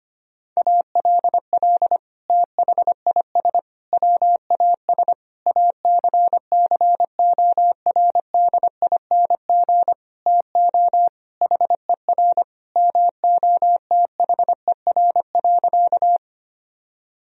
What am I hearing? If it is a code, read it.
Morse code, heard as a keyed tone, about 25 wpm, 700 Hz. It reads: ALL THIS WAS ACCORDING TO HER MOTHER.
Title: Sentences from Top 1000 Words - Review Part 4 - 25wpm